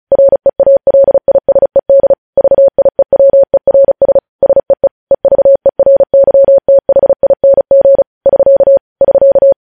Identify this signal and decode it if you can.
Morse code, heard as a keyed tone, about 36 wpm, 550 Hz. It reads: REALISED VIEWERS SEE EVERYTHING <SK> <SK>
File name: podcast / News headlines-1x-35x35